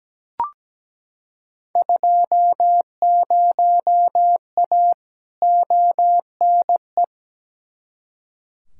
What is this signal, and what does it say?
Morse code, heard as a keyed tone, about 17 wpm, 700 Hz. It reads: 20A ONE
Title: Field Day Exchanges Encore - 17wpm